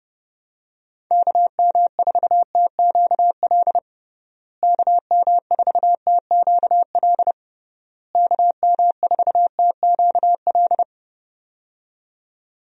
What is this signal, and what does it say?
Morse code, heard as a keyed tone, about 30 wpm, 700 Hz. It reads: KM4TQL KM4TQL KM4TQL
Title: USA Call Signs - 6 Characters - Repeated 3x - 30wpm